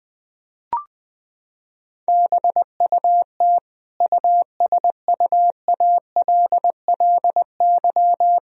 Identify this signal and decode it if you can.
Morse code, heard as a keyed tone, about 20 wpm, 700 Hz. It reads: BUT USUALLY
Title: Sentences from Top 1000 Words - Review Part 3 - 20wpm